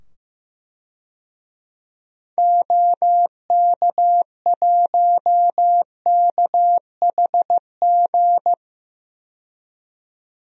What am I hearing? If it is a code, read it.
Morse code, heard as a keyed tone, about 15 wpm, 700 Hz. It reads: OK1KHG
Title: DXCC-100 Call Signs - 6 Characters - 15wpm